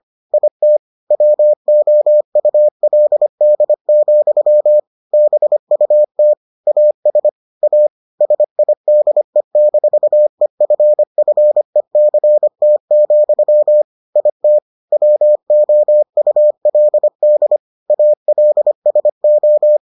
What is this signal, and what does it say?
Morse code, heard as a keyed tone, about 25 wpm, 600 Hz. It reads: IT WOULD, BUT AS A SIDE-EFFECT, IT WOULD ALSO